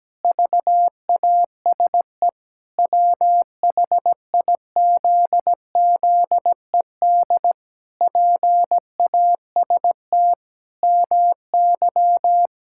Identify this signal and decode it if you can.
Morse code, heard as a keyed tone, about 17 wpm, 700 Hz. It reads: VASE WHIZZED PAST MY
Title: Pangram Challenge v2 - 17wpm